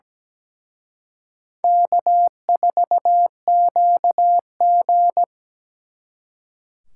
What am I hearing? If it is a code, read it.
Morse code, heard as a keyed tone, about 20 wpm, 700 Hz. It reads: K4QG
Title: Speed Racing - USA 1x2 Call Signs 17wpm